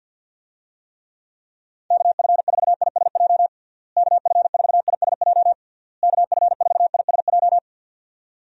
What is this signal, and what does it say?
Morse code, heard as a keyed tone, about 46 wpm, 700 Hz. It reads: KF4ISJ KF4ISJ KF4ISJ
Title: USA Call Signs - 6 Characters - Repeated 3x - 50wpm